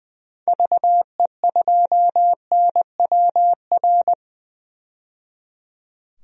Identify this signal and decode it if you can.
Morse code, heard as a keyed tone, about 20 wpm, 700 Hz. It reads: VE2NWR
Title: DXCC-100 Call Signs - 6 Characters - 20wpm